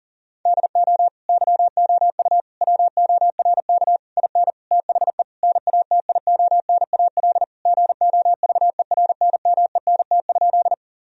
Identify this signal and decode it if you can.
Morse code, heard as a keyed tone, about 37 wpm, 700 Hz. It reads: DO YOU WORK IN THE NATIONAL GOVERNMENT?
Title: Sentences from Top 1000 Words - Review Part 4 - 40wpm